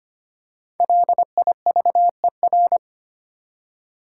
Morse code, heard as a keyed tone, about 25 wpm, 700 Hz. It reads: LIVER
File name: 5 Letter Words - First 3 Letters Given - 25wpm